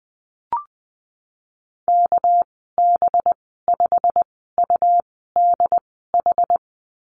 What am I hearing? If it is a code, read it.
Morse code, heard as a keyed tone, about 20 wpm, spaced as for 15 wpm, 700 Hz. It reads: KB5UDH
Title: 207 Lesson 33g - Letter j - Call Signs 20wpm